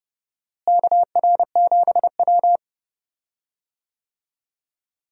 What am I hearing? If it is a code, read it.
Morse code, heard as a keyed tone, about 30 wpm, 700 Hz. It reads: KR7W
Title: USA Call Signs - 4 Characters - Repeated 3x - 30wpm